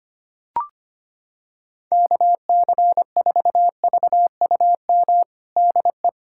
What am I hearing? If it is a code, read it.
Morse code, heard as a keyed tone, about 25 wpm, 700 Hz. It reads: KC4VUM DE
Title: USA Callsign DE Callsign - 6 Characters - 25wpm